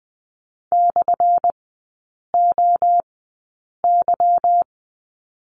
Morse code, heard as a keyed tone, about 20 wpm, spaced as for 8 wpm, 700 Hz. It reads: /OY
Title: 184 Lesson 30e - Letter b - Three Characters 20wpm